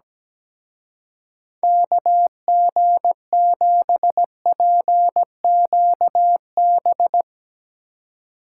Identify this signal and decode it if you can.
Morse code, heard as a keyed tone, about 19 wpm, 700 Hz. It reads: KG7PQB
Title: Speed Racing - USA 2x3 Call Signs 17wpm